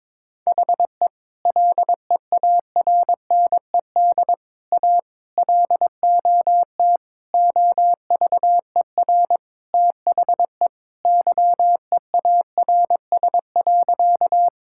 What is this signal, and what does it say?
Morse code, heard as a keyed tone, about 22 wpm, 700 Hz. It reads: HE LEARNED A LOT OVER THE YEARS.
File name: practice / 22wpm / Sentences with 3-Word Phrases - 22wpm